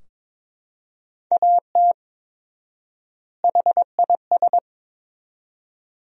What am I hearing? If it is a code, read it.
Morse code, heard as a keyed tone, about 22 wpm, 700 Hz. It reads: AT HIS
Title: 4x Word-Spacing - Sentences from Top 500 Words - 22wpm